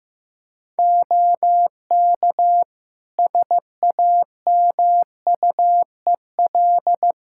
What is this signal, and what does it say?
Morse code, heard as a keyed tone, about 15 wpm, 700 Hz. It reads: OK SAMUEL
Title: Practice QSO - Exchange 2 - Info - 15wpm